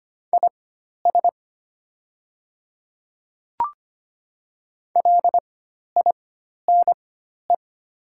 Morse code, heard as a keyed tone, about 25 wpm, spaced as for 12 wpm, 690 Hz. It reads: IS LINE
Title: 4x Character Spacing - Top 300 Words - 25wpm